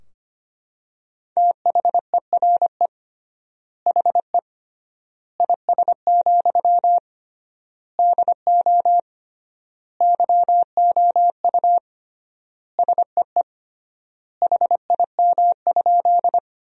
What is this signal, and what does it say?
Morse code, heard as a keyed tone, about 25 wpm, 700 Hz. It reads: THERE HE IS, DO YOU SEE HIM?
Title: 3x Word-Spacing - Sentences from Top 100 Words - 25wpm